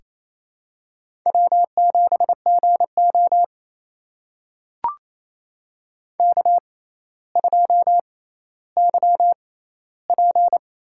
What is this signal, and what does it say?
Morse code, heard as a keyed tone, about 28 wpm, spaced as for 14 wpm, 700 Hz. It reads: W7GO K2YP
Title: USA Call Signs - 4 Characters - 6x Character Spacing - 28wpm